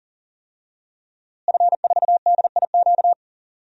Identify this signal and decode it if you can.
Morse code, heard as a keyed tone, about 40 wpm, 700 Hz. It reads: F4DIQ
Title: DXCC-100 Call Signs - 5 Characters - Repeated 3x - 40wpm